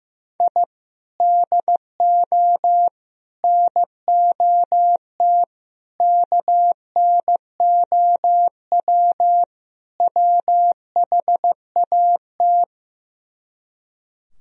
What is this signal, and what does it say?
Morse code, heard as a keyed tone, about 15 wpm, 700 Hz. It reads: I DO NOT KNOW WHAT
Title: Speed-Racing - Sets of 5 Words 15wpm